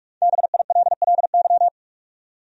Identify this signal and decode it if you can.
Morse code, heard as a keyed tone, about 43 wpm, 700 Hz. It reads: BILLY
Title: USA Common Names - 45wpm